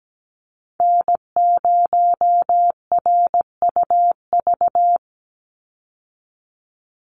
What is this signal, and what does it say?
Morse code, heard as a keyed tone, about 17 wpm, 700 Hz. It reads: N0RUV
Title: USA Call Signs 17wpm